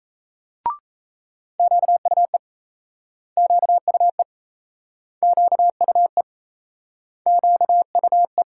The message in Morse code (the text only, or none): QUE QUE QUE QUE